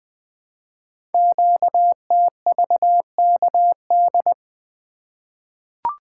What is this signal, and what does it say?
Morse code, heard as a keyed tone, about 20 wpm, 700 Hz. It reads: QTVKD
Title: Code Groups - 5 Letter:Numbers - 20wpm